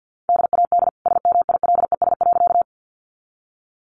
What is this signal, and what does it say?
Morse code, heard as a keyed tone, about 50 wpm, 700 Hz. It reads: DAD SMILES.
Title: Easy Phrases 50wpm